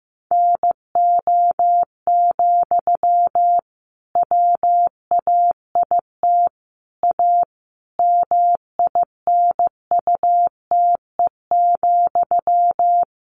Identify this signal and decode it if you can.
Morse code, heard as a keyed tone, about 15 wpm, 700 Hz. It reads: NO, WAIT A MINUTE,
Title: Sentences from Top 1500 Words 15wpm